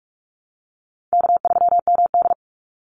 Morse code, heard as a keyed tone, about 46 wpm, 700 Hz. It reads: K3MD
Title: Easy Call Signs - 45wpm